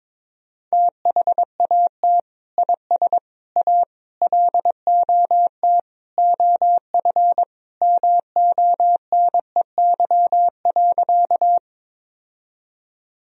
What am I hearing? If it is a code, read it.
Morse code, heard as a keyed tone, about 22 wpm, 700 Hz. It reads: THAT IS A LOT OF MONEY.
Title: Sentences from Top 1000 Words - Review Part 3 - 22wpm